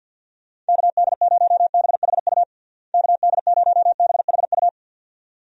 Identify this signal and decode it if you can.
Morse code, heard as a keyed tone, about 50 wpm, 700 Hz. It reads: KD0BHU KD0BHU
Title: USA Call Signs - 6 Characters - Repeated 2x - 50wpm